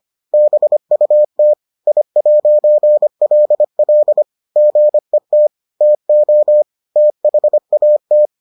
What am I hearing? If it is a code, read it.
Morse code, heard as a keyed tone, about 25 wpm, 600 Hz. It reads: BUT I'LL GET TO THAT